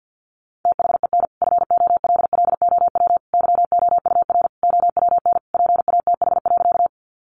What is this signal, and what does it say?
Morse code, heard as a keyed tone, about 50 wpm, 700 Hz. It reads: THEN FOLLOW YOUR OWN PATH.
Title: Sentences from Top 1400 Words 50wpm